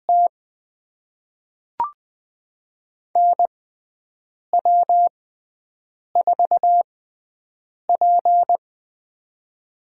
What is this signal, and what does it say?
Morse code, heard as a keyed tone, about 20 wpm, spaced as for 7 wpm, 700 Hz. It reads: T NW4P
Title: USA Call Signs - 4 Characters - 6x Character Spacing - 20wpm